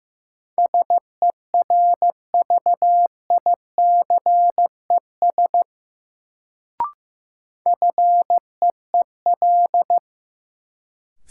Morse code, heard as a keyed tone, about 15 wpm, 700 Hz. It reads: SERVICES FEEL
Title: Sets of 5 Words - New Word Drill - 15wpm